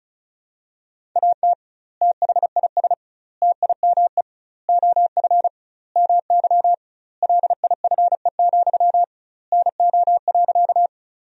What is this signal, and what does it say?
Morse code, heard as a keyed tone, about 34 wpm, 700 Hz. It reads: AT THIS TIME OF MY LIFE, NO.
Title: 2x Word-Spacing - Sentences from Top 300 Words - 35wpm